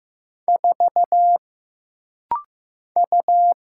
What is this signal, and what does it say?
Morse code, heard as a keyed tone, about 15 wpm, 700 Hz. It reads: 4 U
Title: Single Letter-Number Alt - Rapid-Fire 15wpm